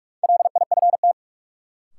Morse code, heard as a keyed tone, about 47 wpm, 700 Hz. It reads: LIFT
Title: Sentences from Top 1000 Words - New Word Drill - 45wpm